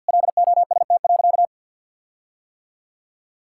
Morse code, heard as a keyed tone, about 50 wpm, 700 Hz. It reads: LOST.
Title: Sentences from Top 1600 Words 50wpm